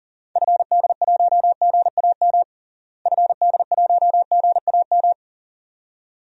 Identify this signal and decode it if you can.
Morse code, heard as a keyed tone, about 40 wpm, 700 Hz. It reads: FD1GAM FD1GAM
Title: DXCC-100 Call Signs - 6 Characters - Repeated 2x - 40wpm